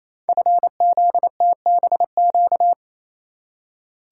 Morse code, heard as a keyed tone, about 29 wpm, 700 Hz. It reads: FZTBQ
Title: Code Groups - 5 Letters - 28wpm